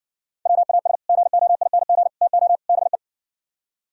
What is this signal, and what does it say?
Morse code, heard as a keyed tone, about 59 wpm, 700 Hz. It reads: WAS GOING TO BE